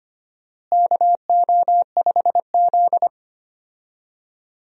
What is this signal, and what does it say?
Morse code, heard as a keyed tone, about 25 wpm, 700 Hz. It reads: KO5Z
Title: USA Call Signs - 4 Characters - Repeated 3x - 25wpm